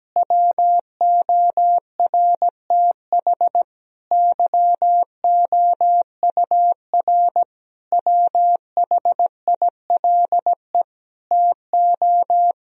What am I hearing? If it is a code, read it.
Morse code, heard as a keyed tone, about 17 wpm, 700 Hz. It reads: WORTH YOUR WHILE TO
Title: Sentences from Top 1000 Words - Review Part 1 - 17wpm